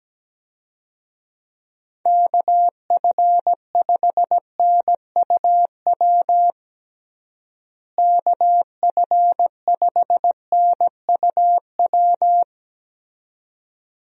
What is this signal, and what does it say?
Morse code, heard as a keyed tone, about 17 wpm, 700 Hz. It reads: KF5NUW KF5NUW
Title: USA Call Signs - 6 Characters - Repeated 2x - 17wpm